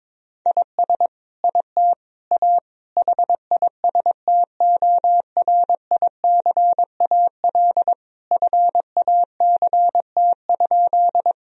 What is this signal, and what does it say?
Morse code, heard as a keyed tone, about 22 wpm, 700 Hz. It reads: IS IT A HISTORICAL FACT?
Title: Sentences from Top 1000 Words - Review Part 4 - 22wpm